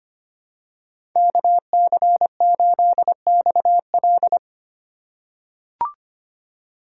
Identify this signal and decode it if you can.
Morse code, heard as a keyed tone, about 25 wpm, 700 Hz. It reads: KC8XL
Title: Code Groups - 5 Letter:Numbers - 25wpm